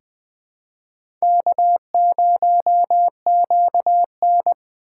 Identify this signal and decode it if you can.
Morse code, heard as a keyed tone, about 21 wpm, 700 Hz. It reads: K0QN
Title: USA Call Signs - 2x Character Spacing - 20wpm